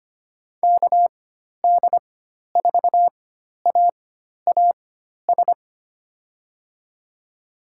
Morse code, heard as a keyed tone, about 25 wpm, spaced as for 12 wpm, 700 Hz. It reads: KD4AAS
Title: USA Call Signs - 6 Characters - 4x Character Spacing - 25wpm